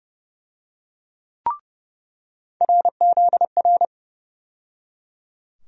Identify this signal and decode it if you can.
Morse code, heard as a keyed tone, about 32 wpm, 700 Hz. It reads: RZR
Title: USA Call Sign Postfixes - 3 Letters 30wpm